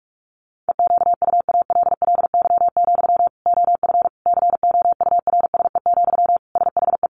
Morse code, heard as a keyed tone, about 44 wpm, 700 Hz. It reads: EQUALLY, OF COURSE, SHE
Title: Sentences from Top 1400 Words 45wpm